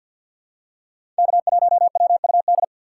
Morse code, heard as a keyed tone, about 52 wpm, 700 Hz. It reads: K1WUD
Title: USA Call Signs - 5 Characters - 6x Character Spacing - 50wpm